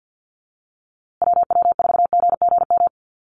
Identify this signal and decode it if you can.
Morse code, heard as a keyed tone, about 50 wpm, 700 Hz. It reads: WW4GGM
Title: USA Call Signs Repeated 2x 50wpm